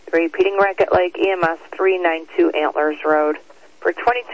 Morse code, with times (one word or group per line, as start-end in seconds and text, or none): none